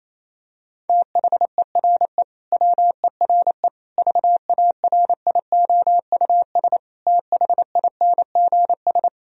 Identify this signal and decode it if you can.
Morse code, heard as a keyed tone, about 28 wpm, 700 Hz. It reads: THERE WERE VARIOUS THINGS
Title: Sentences from Top 1000 Words - Review Part 4 - 28wpm